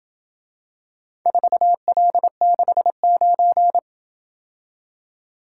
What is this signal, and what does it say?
Morse code, heard as a keyed tone, about 27 wpm, 700 Hz. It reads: 4L69